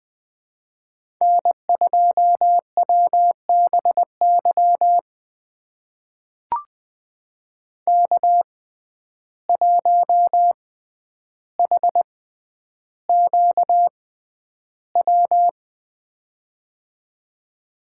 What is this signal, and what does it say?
Morse code, heard as a keyed tone, about 20 wpm, spaced as for 10 wpm, 700 Hz. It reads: N2WBY K1HQW
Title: USA Call Signs - 5 Characters - 6x Character Spacing - 20wpm